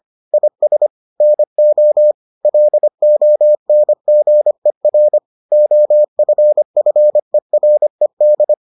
Morse code, heard as a keyed tone, about 25 wpm, 600 Hz. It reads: IS NO LONGER OFFERED